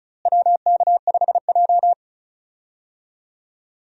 Morse code, heard as a keyed tone, about 36 wpm, 700 Hz. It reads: WK5J
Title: USA Call Signs - 4 Characters - Repeated 3x - 35wpm